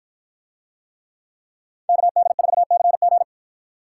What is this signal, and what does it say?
Morse code, heard as a keyed tone, about 51 wpm, 700 Hz. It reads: KD4CG